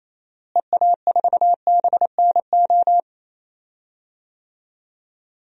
Morse code, heard as a keyed tone, about 28 wpm, 700 Hz. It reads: EA4BNO
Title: DXCC-100 Call Signs - 6 Characters - Repeated 2x - 28wpm